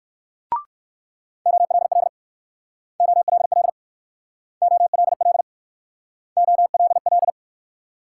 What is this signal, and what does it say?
Morse code, heard as a keyed tone, about 60 wpm, 700 Hz. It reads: OLL OLL OLL OLL